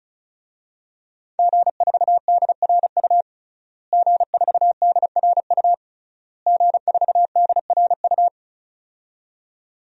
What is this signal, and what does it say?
Morse code, heard as a keyed tone, about 35 wpm, 700 Hz. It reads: G4DRU G4DRU G4DRU
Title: DXCC-100 Call Signs - 5 Characters - Repeated 3x - 35wpm